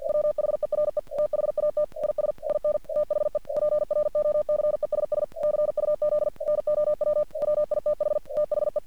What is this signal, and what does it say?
Morse code, heard as a keyed tone, about 48 wpm, 600 Hz. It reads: WHERE THAT IS IN THE PROCESS CUZ NOW WITH THE
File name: archived / CRSnet